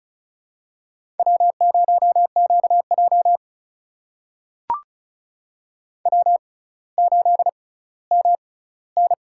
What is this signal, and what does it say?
Morse code, heard as a keyed tone, about 36 wpm, spaced as for 18 wpm, 700 Hz. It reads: W0QJ W8MN
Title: USA Call Signs - 4 Characters - 6x Character Spacing - 35wpm